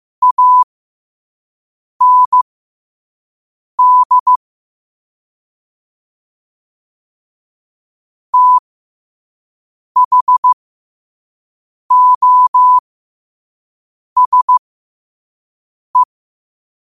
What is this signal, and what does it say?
Morse code, heard as a keyed tone, about 15 wpm, spaced as for 5 wpm, 1000 Hz. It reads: AND THOSE
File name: qotdfiles / 5wpm